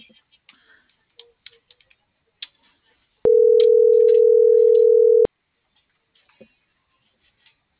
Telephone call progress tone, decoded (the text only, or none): ringback tone